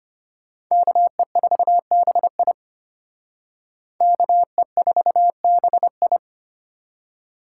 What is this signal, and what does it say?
Morse code, heard as a keyed tone, about 32 wpm, 700 Hz. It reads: KE4BI KE4BI